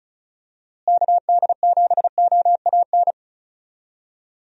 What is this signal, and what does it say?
Morse code, heard as a keyed tone, about 34 wpm, 700 Hz. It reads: KD7OAN
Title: USA Call Signs - 6 Characters - Repeated 3x - 35wpm